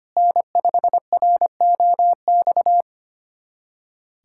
Morse code, heard as a keyed tone, about 25 wpm, 700 Hz. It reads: N5ROX
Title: USA Callsign DE Callsign - 5 Characters - 25wpm